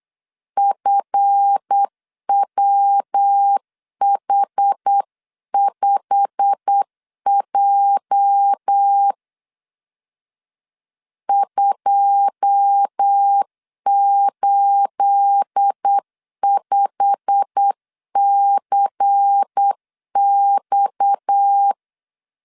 Morse code, heard as a keyed tone, about 8 wpm, 800 Hz. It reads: FWH5J 285CX
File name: CW / DET / 1 9 1500C S C 8wpm 5653